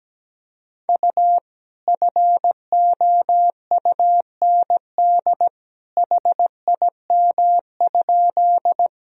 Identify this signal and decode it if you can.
Morse code, heard as a keyed tone, about 17 wpm, 700 Hz. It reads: U FOUND HIM?
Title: Sentences from Top 300 Words and Abbreviations 17wpm